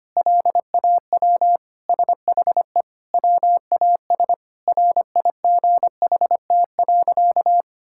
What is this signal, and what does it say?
Morse code, heard as a keyed tone, about 25 wpm, 700 Hz. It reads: LAW SHE WAS RIGHT.
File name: practice / 25wpm / Sentences with 4-Word Phrases - 25wpm